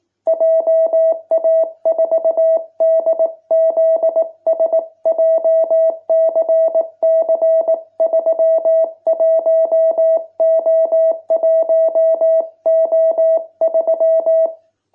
Morse code, heard as a keyed tone, about 18 wpm, 660 Hz. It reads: JA4DZS JCC31O1O3